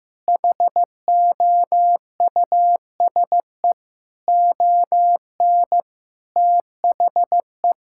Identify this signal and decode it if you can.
Morse code, heard as a keyed tone, about 15 wpm, 700 Hz. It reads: HOUSE ON THE
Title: Sentences from Top 1600 Words 15wpm